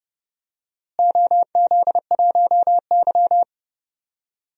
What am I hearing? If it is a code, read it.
Morse code, heard as a keyed tone, about 31 wpm, 700 Hz. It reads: OZ1Y
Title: DXCC-100 Call Signs - 4 Characters - Repeated 3x - 30wpm